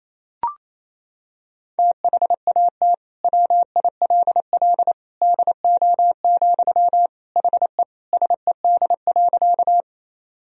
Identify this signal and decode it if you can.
Morse code, heard as a keyed tone, about 27 wpm, 700 Hz. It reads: THAT WILL DO, HE SED.
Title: Sentences from Top 100 Words and Abbreviations 28wpm